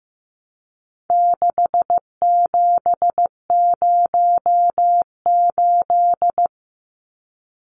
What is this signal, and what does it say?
Morse code, heard as a keyed tone, about 15 wpm, 700 Hz. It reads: 6708
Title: Numbers - 4-Digit 15wpm